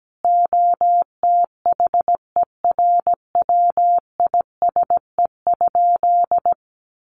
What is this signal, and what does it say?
Morse code, heard as a keyed tone, about 17 wpm, 700 Hz. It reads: OTHERWISE?
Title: Sentences from Top 1200 Words 17wpm